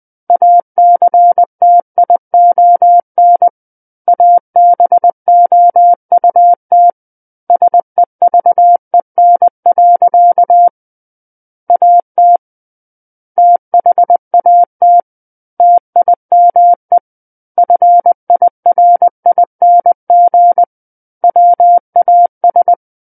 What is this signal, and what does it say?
Morse code, heard as a keyed tone, about 20 wpm, 700 Hz. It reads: ACTION ABOUT SEVEN. AT THAT TIME FIRING WAS
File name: 20 WPM